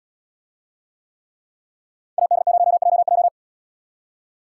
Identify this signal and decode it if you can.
Morse code, heard as a keyed tone, about 75 wpm, 700 Hz. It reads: AD0JJ